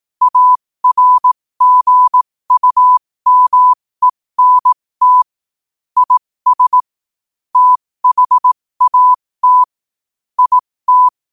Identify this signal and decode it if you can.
Morse code, heard as a keyed tone, about 18 wpm, spaced as for 15 wpm, 1000 Hz. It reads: ARGUMENT IS THAT IT
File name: qotdfiles / 15wpm